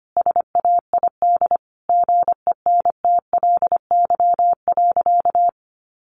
Morse code, heard as a keyed tone, about 25 wpm, 700 Hz. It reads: SAID GENTLY.
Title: Sentences from Top 1500 Words 25wpm